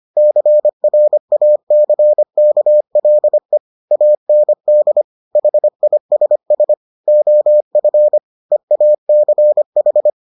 Morse code, heard as a keyed tone, about 25 wpm, 600 Hz. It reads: CRACKLE AND HISS OF EACH